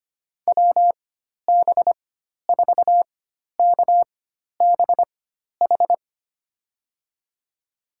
Morse code, heard as a keyed tone, about 25 wpm, spaced as for 12 wpm, 700 Hz. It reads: WB4KBH
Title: USA Call Signs - 6 Characters - 4x Character Spacing - 25wpm